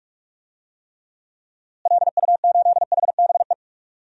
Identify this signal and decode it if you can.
Morse code, heard as a keyed tone, about 41 wpm, 700 Hz. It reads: LU8HBE
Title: DXCC-100 Call Signs - 6 Characters - Repeated 3x - 45wpm